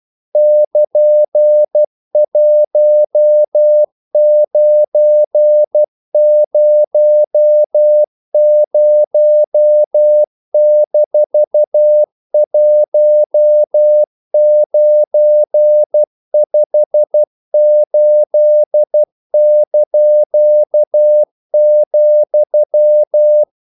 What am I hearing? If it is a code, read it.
Morse code, heard as a keyed tone, about 12 wpm, 600 Hz. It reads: <KN>1900-1958),
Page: English quotations in CW (telegraphy - morse)